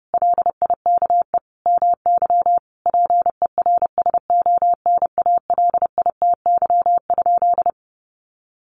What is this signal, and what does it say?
Morse code, heard as a keyed tone, about 30 wpm, 700 Hz. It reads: LIKE MY PERSONALITY?
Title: Sentences from Top 1400 Words 30wpm